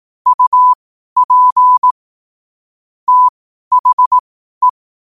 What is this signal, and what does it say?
Morse code, heard as a keyed tone, about 18 wpm, spaced as for 12 wpm, 1000 Hz. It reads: UP THE
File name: qotdfiles / 12wpm